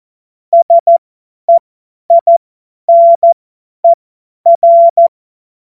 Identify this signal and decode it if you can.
Morse code, heard as a keyed tone, about 14 wpm, spaced as for 10 wpm, 680 Hz. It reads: SEINER
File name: Morsekurs / Witze / Witz 18. April 2023 50 F70